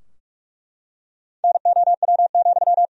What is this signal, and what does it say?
Morse code, heard as a keyed tone, about 46 wpm, 700 Hz. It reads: NOW,
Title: Sentences from Top 1000 Words - Review Part 3 - 45wpm